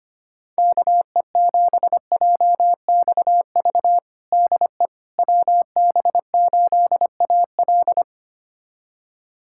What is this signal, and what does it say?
Morse code, heard as a keyed tone, about 25 wpm, 700 Hz. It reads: KE7JXV DE WB8AL
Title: USA Callsign DE Callsign 25wpm